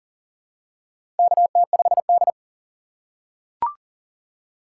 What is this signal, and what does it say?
Morse code, heard as a keyed tone, about 40 wpm, 700 Hz. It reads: KT5D
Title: USA Call Signs - 4 Characters - 8x Character Spacing - 40wpm